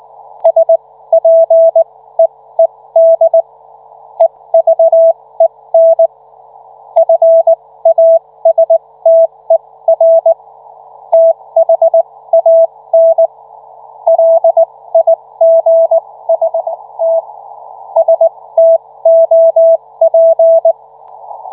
Morse code, heard as a keyed tone, about 19 wpm, spaced as for 15 wpm, 660 Hz. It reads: SPEED EVEN FASTER THAN LIGST STOP